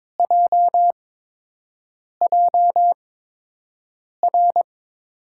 Morse code, heard as a keyed tone, about 22 wpm, spaced as for 6 wpm, 700 Hz. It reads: JJR
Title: USA Call Signs - 6 Characters - 8x Character Spacing - 22wpm